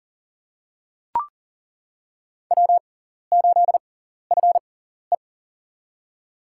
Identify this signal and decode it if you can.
Morse code, heard as a keyed tone, about 40 wpm, spaced as for 14 wpm, 700 Hz. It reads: W8FE
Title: USA Call Signs - 6x Character Spacing - 40wpm